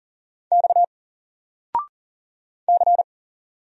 Morse code, heard as a keyed tone, about 40 wpm, 700 Hz. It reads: X C
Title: Single Letter-Number 40wpm